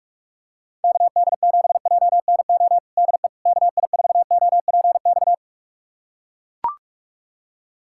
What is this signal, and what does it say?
Morse code, heard as a keyed tone, about 42 wpm, 700 Hz. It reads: KD7JNO DE KI4OPX
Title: USA Callsign DE Callsign 45wpm